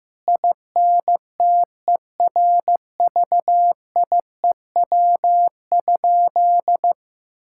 Morse code, heard as a keyed tone, about 15 wpm, 700 Hz. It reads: INTERVIEW?
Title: Sentences from Top 1600 Words 15wpm